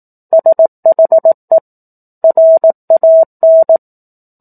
Morse code, heard as a keyed tone, about 18 wpm, 650 Hz. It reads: SHE RAN
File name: 04-Chapter-04 18 WPM